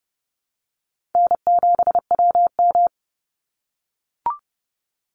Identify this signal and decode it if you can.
Morse code, heard as a keyed tone, about 30 wpm, 700 Hz. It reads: N7WM
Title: NA SOTA Chaser Callsigns 30wpm